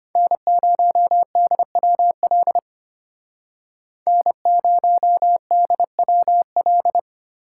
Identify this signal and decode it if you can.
Morse code, heard as a keyed tone, about 32 wpm, 700 Hz. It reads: N0DWL N0DWL